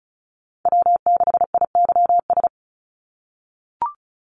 Morse code, heard as a keyed tone, about 34 wpm, 700 Hz. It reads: W6IYS
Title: Call Signs - 35wpm